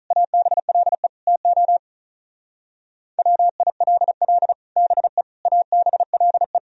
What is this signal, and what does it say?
Morse code, heard as a keyed tone, about 42 wpm, 700 Hz. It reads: ABLE TO WILL BE ABLE
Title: Speed-Racing - Sets of 4 Words 35wpm